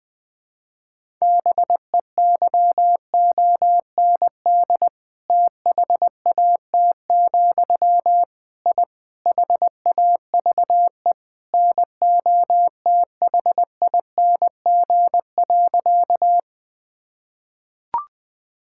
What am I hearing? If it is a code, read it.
Morse code, heard as a keyed tone, about 20 wpm, 700 Hz. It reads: BEYOND THAT, I HAVE NOTHING.
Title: Sentences from Top 1000 Words - Review Part 6 - 20wpm